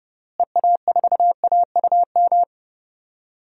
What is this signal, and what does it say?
Morse code, heard as a keyed tone, about 30 wpm, 700 Hz. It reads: EA4AUM
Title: DXCC-100 Call Signs - 6 Characters - Repeated 3x - 30wpm